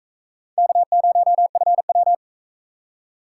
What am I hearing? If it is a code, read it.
Morse code, heard as a keyed tone, about 39 wpm, 700 Hz. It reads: K0FW